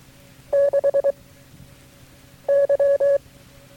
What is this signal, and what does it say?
Morse code, heard as a keyed tone, about 23 wpm, 560 Hz. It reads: 6 Y
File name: FreeFiles / CW-practice / KB6MT-tape3-side1